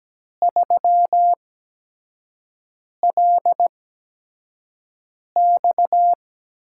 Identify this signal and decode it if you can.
Morse code, heard as a keyed tone, about 17 wpm, spaced as for 5 wpm, 700 Hz. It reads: 3LX
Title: USA Call Signs - 4 Characters - 8x Character Spacing - 17wpm